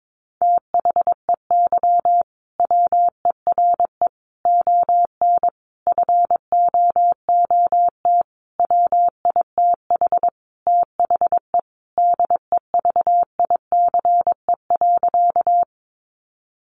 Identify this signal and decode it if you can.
Morse code, heard as a keyed tone, about 22 wpm, 700 Hz. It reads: THEY WERE ON FOOT WITH THE DEVICE.
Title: Sentences from Top 1500 Words 22wpm